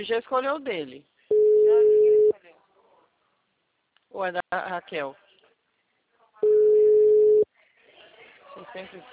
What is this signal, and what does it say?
Ringback tone (Europe, 425 Hz), with speech.